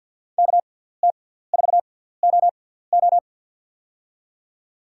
Morse code, heard as a keyed tone, about 52 wpm, spaced as for 17 wpm, 700 Hz. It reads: KT4OO
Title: USA Call Signs - 5 Characters - 6x Character Spacing - 50wpm